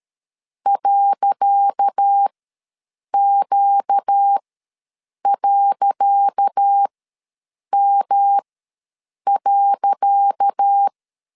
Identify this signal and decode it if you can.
Morse code, heard as a keyed tone, about 13 wpm, spaced as for 7 wpm, 800 Hz. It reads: .Q.M.
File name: LEZIONI / 10 SIGNS / 10C DOT ws 12wpm 3548